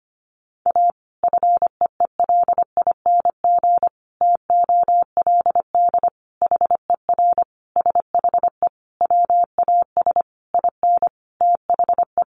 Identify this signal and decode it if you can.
Morse code, heard as a keyed tone, about 25 wpm, 700 Hz. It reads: A FEELING TOLD HER SHE WAS IN THE
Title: Sentences from Top 1200 Words 25wpm